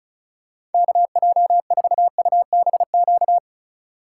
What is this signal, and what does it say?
Morse code, heard as a keyed tone, about 36 wpm, 700 Hz. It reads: KJ4UBQ
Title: USA Call Signs - 6 Characters - 2x Character Spacing - 35wpm